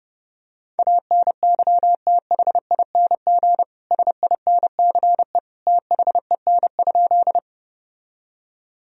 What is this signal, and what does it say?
Morse code, heard as a keyed tone, about 30 wpm, 700 Hz. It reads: ANYTHING SINCE THEN?
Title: Sentences from Top 1000 Words - Review Part 1 - 30wpm